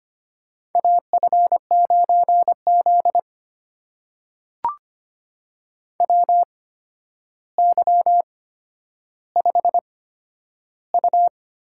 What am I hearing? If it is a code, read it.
Morse code, heard as a keyed tone, about 25 wpm, spaced as for 11 wpm, 700 Hz. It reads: AF9Z WY5U
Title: USA Call Signs - 4 Characters - 8x Character Spacing - 25wpm